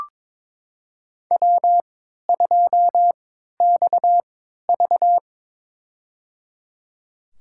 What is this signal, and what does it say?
Morse code, heard as a keyed tone, about 21 wpm, spaced as for 12 wpm, 700 Hz. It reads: W2XV
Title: USA Call Signs - 3x Character Spacing - 22wpm